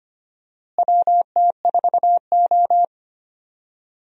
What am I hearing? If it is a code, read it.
Morse code, heard as a keyed tone, about 25 wpm, 700 Hz. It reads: WT4O
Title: USA Call Signs - 4 Characters - Repeated 2x - 25wpm